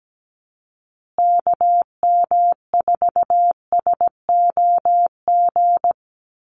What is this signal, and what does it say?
Morse code, heard as a keyed tone, about 17 wpm, 700 Hz. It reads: KM4SOG
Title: USA Call Signs Repeated 3x 17wpm